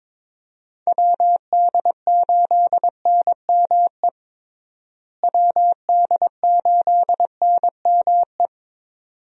Morse code, heard as a keyed tone, about 22 wpm, 700 Hz. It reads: WD8NME WD8NME
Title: USA Call Signs - 6 Characters - Repeated 2x - 22wpm